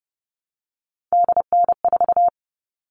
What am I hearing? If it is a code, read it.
Morse code, heard as a keyed tone, about 30 wpm, 700 Hz. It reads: DN4
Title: All Cut Numbers - 3-Digit 30wpm